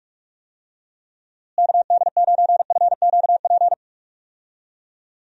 Morse code, heard as a keyed tone, about 46 wpm, 700 Hz. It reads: KD9FQP
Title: USA Call Signs - 6 Characters - Repeated 2x - 45wpm